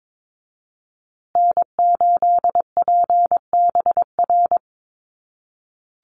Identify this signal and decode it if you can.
Morse code, heard as a keyed tone, about 22 wpm, 700 Hz. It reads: N8PBR
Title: USA Call Signs Repeated 2x 22wpm